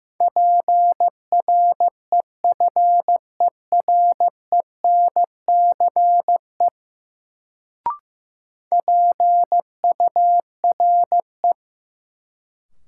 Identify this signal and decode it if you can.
Morse code, heard as a keyed tone, about 15 wpm, 700 Hz. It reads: PREFERENCE PURE
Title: Sentences from Top 1500 Words - New Word Drill - 15wpm